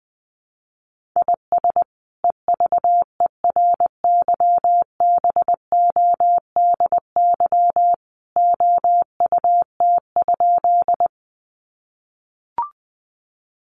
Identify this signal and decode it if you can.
Morse code, heard as a keyed tone, about 20 wpm, 700 Hz. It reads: IS EVERYBODY OUT?
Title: Sentences from Top 1400 Words 20wpm